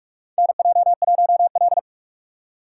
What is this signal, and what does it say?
Morse code, heard as a keyed tone, about 45 wpm, 700 Hz. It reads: NJ1L
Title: USA Call Signs - 4 Characters - 4x Character Spacing - 45wpm